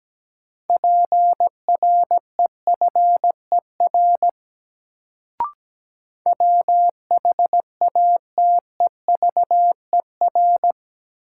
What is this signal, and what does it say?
Morse code, heard as a keyed tone, about 17 wpm, 700 Hz. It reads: PREFER WHATEVER
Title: Sentences from Top 1100 Words - New Word Drill Alt - 17wpm